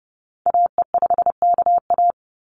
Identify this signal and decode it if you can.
Morse code, heard as a keyed tone, about 30 wpm, 700 Hz. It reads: AE5KA
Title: NA SOTA Activator Callsigns 30wpm